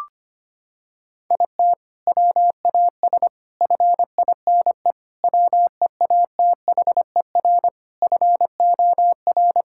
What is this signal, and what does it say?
Morse code, heard as a keyed tone, about 25 wpm, 700 Hz. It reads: IT WAS FINE WEATHER FOR
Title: Sentences from Top 1000 Words - Review Part 5 - 25wpm